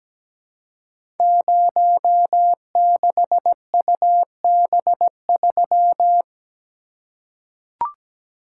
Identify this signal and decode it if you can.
Morse code, heard as a keyed tone, about 17 wpm, 700 Hz. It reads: 06UB3
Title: Code Groups - 5 Letter:Numbers - 17wpm